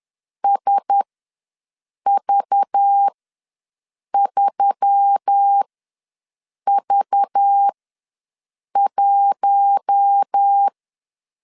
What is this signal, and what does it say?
Morse code, heard as a keyed tone, about 10 wpm, spaced as for 6 wpm, 800 Hz. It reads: SV3V1
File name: LEZIONI / 5 LBVC / 5 LBV ws 10wpm 3645